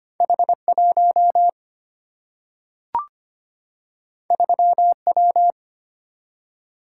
Morse code, heard as a keyed tone, about 25 wpm, 700 Hz. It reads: H 1 3 W
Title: Single-Letter Single-Number 25wpm